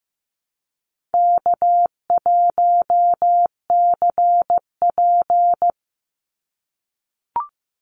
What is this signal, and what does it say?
Morse code, heard as a keyed tone, about 15 wpm, 700 Hz. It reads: K1CP
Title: Easy Call Signs - 15wpm